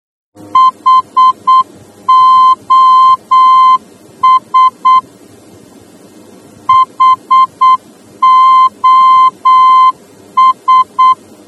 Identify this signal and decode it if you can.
Morse code, heard as a keyed tone, about 8 wpm, 1040 Hz. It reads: HOS HOS
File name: germany / mx / HOS-D